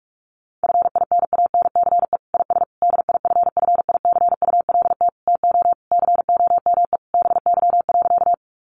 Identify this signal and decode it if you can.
Morse code, heard as a keyed tone, about 44 wpm, 700 Hz. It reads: FINANCE IS DIFFICULT TO COME BY.
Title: Sentences from Top 1500 Words 45wpm